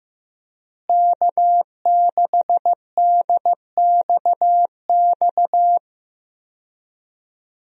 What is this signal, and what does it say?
Morse code, heard as a keyed tone, about 15 wpm, 700 Hz. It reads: K6DXX
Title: USA Call Signs - 5 Characters - Repeated 3x - 15wpm